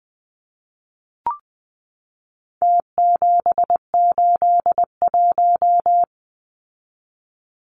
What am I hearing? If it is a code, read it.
Morse code, heard as a keyed tone, about 20 wpm, 700 Hz. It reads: T781
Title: Numbers - Contest Style - 4-Digit 20wpm